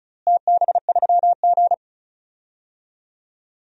Morse code, heard as a keyed tone, about 34 wpm, 700 Hz. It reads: TB3G
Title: DXCC-100 Call Signs - 4 Characters - Repeated 2x - 35wpm